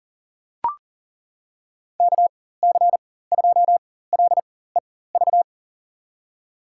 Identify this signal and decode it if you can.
Morse code, heard as a keyed tone, about 40 wpm, spaced as for 19 wpm, 690 Hz. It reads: KC2LEV
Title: USA Call Signs - 4x Character Spacing - 40wpm